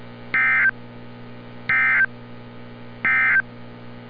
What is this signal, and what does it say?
Emergency Alert System (SAME) tones, heard as harsh buzzing bursts. It is heard from 0.2 to 3.5 s.
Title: NOAA Weather Radio - Altoona, PA